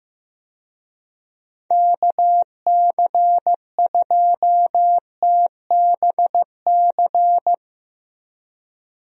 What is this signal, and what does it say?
Morse code, heard as a keyed tone, about 15 wpm, 700 Hz. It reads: KC2TBC
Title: USA Call Signs - 6 Characters - Repeated 3x - 15wpm